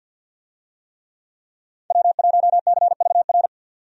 Morse code, heard as a keyed tone, about 52 wpm, 700 Hz. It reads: W1CVR
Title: USA Call Signs - 5 Characters - 50wpm